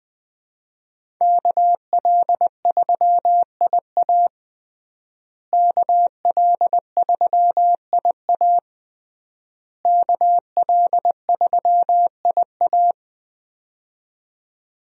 Morse code, heard as a keyed tone, about 20 wpm, 700 Hz. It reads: KL3IA KL3IA KL3IA
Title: USA Call Signs - 5 Characters - Repeated 3x - 20wpm